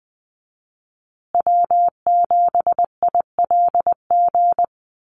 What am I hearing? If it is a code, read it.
Morse code, heard as a keyed tone, about 20 wpm, 700 Hz. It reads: W7ILG
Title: USA Call Signs Repeated 3x 20wpm